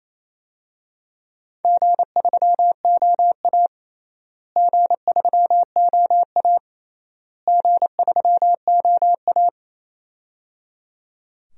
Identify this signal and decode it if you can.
Morse code, heard as a keyed tone, about 28 wpm, 700 Hz. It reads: G3OA G3OA G3OA
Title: DXCC-100 Call Signs - 4 Characters - Repeated 3x - 28wpm